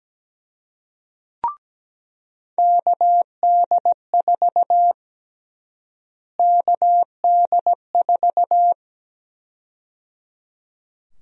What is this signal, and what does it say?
Morse code, heard as a keyed tone, about 17 wpm, 700 Hz. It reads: KD4 KD4
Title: USA Call Sign Prefixes - Repeated 2x - 17wpm